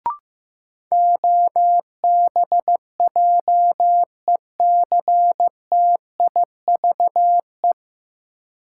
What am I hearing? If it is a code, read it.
Morse code, heard as a keyed tone, about 15 wpm, 700 Hz. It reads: OBJECTIVE
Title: Sentences from Top 1200 Words - New Word Drill Alt - 15wpm